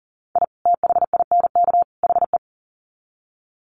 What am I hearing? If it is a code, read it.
Morse code, heard as a keyed tone, about 40 wpm, 700 Hz. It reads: I THINK HE
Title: Sets of 3 Words 40wpm